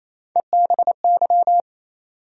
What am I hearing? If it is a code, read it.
Morse code, heard as a keyed tone, about 28 wpm, 700 Hz. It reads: EBY
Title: USA Call Sign Postfixes - 3 Letters 28wpm